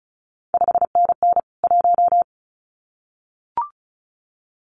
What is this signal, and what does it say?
Morse code, heard as a keyed tone, about 36 wpm, 700 Hz. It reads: 5NN 1
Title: CQ WW DX Contest Exchange Encore - 35wpm